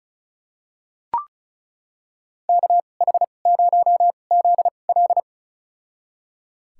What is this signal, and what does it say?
Morse code, heard as a keyed tone, about 34 wpm, spaced as for 25 wpm, 700 Hz. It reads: KH0ZL
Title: USA Call Signs - 5 Characters - 2x Character Spacing - 35wpm